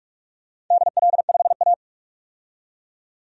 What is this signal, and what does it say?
Morse code, heard as a keyed tone, about 46 wpm, 700 Hz. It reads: DL5A
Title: DXCC-100 Call Signs - 4 Characters - 45wpm